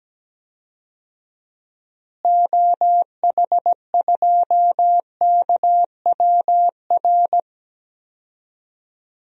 Morse code, heard as a keyed tone, about 17 wpm, 700 Hz. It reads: OH2KWR
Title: DXCC-100 Call Signs - 6 Characters - Repeated 2x - 17wpm